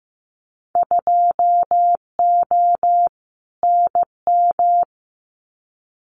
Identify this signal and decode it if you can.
Morse code, heard as a keyed tone, about 15 wpm, 700 Hz. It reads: 2O NM
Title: Winter Field Day Exchanges 15wpm